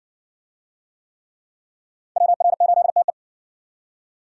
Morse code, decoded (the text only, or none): WR8TE